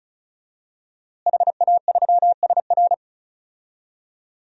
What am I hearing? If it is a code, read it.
Morse code, heard as a keyed tone, about 34 wpm, 700 Hz. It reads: HA3SR
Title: DXCC-100 Call Signs - 5 Characters - Repeated 3x - 35wpm